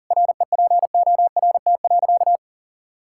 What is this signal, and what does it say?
Morse code, heard as a keyed tone, about 40 wpm, 700 Hz. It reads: REPORT.
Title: Sentences from Top 1000 Words - Review Part 6 - 40wpm